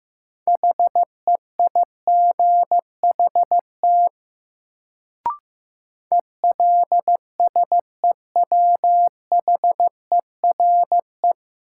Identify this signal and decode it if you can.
Morse code, heard as a keyed tone, about 15 wpm, 700 Hz. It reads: HEIGHT ELSEWHERE
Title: Sentences from Top 1600 Words - New Word Drill Alt - 15wpm